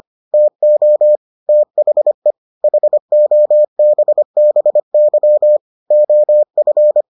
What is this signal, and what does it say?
Morse code, heard as a keyed tone, about 25 wpm, 600 Hz. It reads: TO THE HOBBY OF